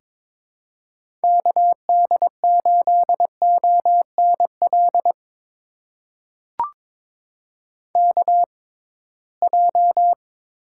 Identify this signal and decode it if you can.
Morse code, heard as a keyed tone, about 22 wpm, spaced as for 17 wpm, 700 Hz. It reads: KD8ONL KJ
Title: USA Call Signs - 6 Characters - 6x Character Spacing - 22wpm